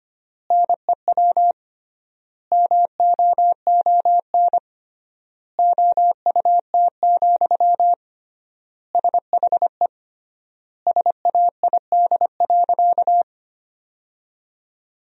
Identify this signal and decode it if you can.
Morse code, heard as a keyed tone, about 25 wpm, 700 Hz. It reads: NEW MOON OUT, SHE SAID.
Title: 3x Word-Spacing - Sentences from Top 500 Words - 25wpm